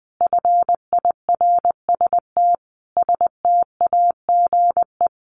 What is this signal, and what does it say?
Morse code, heard as a keyed tone, about 20 wpm, 700 Hz. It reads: FIRST STAGE
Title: Sentences from Top 1200 Words 20wpm